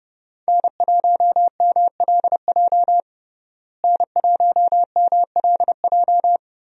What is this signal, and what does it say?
Morse code, heard as a keyed tone, about 32 wpm, 700 Hz. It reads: N1MLJ N1MLJ
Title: USA Call Signs - 5 Characters - Repeated 3x - 30wpm